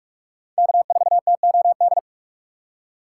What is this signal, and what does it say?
Morse code, heard as a keyed tone, about 45 wpm, 700 Hz. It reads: K4TOD
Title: USA Call Signs - 5 Characters - 2x Character Spacing - 45wpm